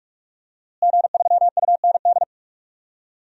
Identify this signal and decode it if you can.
Morse code, heard as a keyed tone, about 41 wpm, 700 Hz. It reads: G3UND
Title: DXCC-100 Call Signs - 5 Characters - Repeated 3x - 45wpm